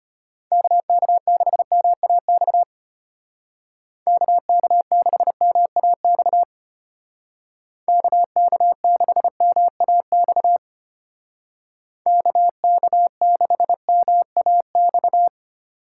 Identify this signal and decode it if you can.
Morse code, heard as a keyed tone, about 31 wpm, 700 Hz. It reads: KK6MAX KK6MAX KK6MAX KK6MAX